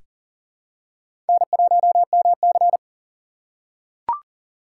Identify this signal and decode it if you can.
Morse code, heard as a keyed tone, about 40 wpm, 700 Hz. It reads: N1MC
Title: USA Call Signs - 4 Characters - 3x Character Spacing - 40wpm